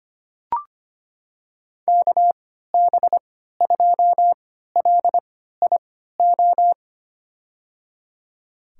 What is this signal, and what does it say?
Morse code, heard as a keyed tone, about 25 wpm, spaced as for 14 wpm, 700 Hz. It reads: KB2LIO
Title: USA Call Signs - 6 Characters - 3x Character Spacing - 25wpm